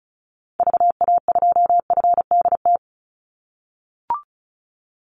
Morse code, heard as a keyed tone, about 34 wpm, 700 Hz. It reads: VA2FDT
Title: Call Signs - 35wpm